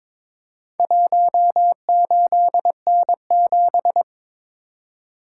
Morse code, heard as a keyed tone, about 22 wpm, 700 Hz. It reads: 18N7
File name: practice / 22wpm / Four Letter-Number Combinations - 22wpm